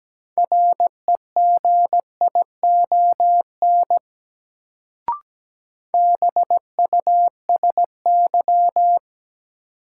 Morse code, heard as a keyed tone, about 17 wpm, 700 Hz. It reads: REGION BUSY
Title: Sentences from Top 1500 Words - New Word Drill Alt - 17wpm